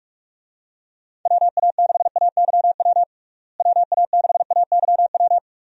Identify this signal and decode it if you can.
Morse code, heard as a keyed tone, about 44 wpm, 700 Hz. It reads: WA6AYW WA6AYW
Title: USA Call Signs - 6 Characters - Repeated 2x - 45wpm